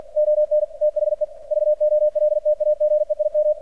Morse code, heard as a keyed tone, about 50 wpm, 600 Hz, with radio noise. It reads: ON THE VOLTAGE IM
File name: archived / CRSnet